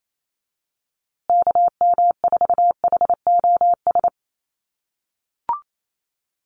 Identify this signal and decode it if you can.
Morse code, heard as a keyed tone, about 28 wpm, 700 Hz. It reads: KM4HOS
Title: USA Call Signs Repeated 3x 28wpm